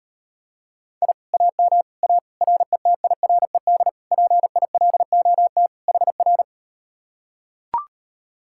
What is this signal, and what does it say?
Morse code, heard as a keyed tone, about 39 wpm, 700 Hz. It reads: I AM A RETIRED PILOT HR